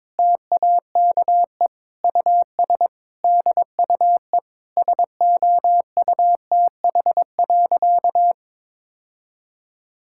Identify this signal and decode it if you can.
Morse code, heard as a keyed tone, about 22 wpm, 700 Hz. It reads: TAKE US DUE SOUTH.
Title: Sentences from Top 1000 Words - Review Part 6 - 22wpm